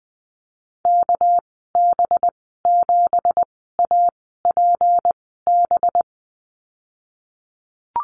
Morse code, heard as a keyed tone, about 20 wpm, 700 Hz. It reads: K B 7 A P B
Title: 238 Lesson 38g - Letter q - Call Signs 20wpm